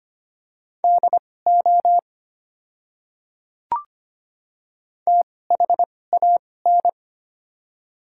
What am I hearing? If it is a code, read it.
Morse code, heard as a keyed tone, about 25 wpm, spaced as for 18 wpm, 700 Hz. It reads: DO THAN
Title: 2x Character Spacing - Top 100 Words - 25wpm